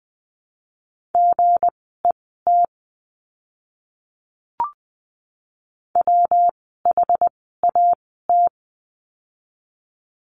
Morse code, heard as a keyed tone, about 20 wpm, spaced as for 14 wpm, 700 Hz. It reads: GET WHAT
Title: 206 Lesson 33f - Letter j - Words 20wpm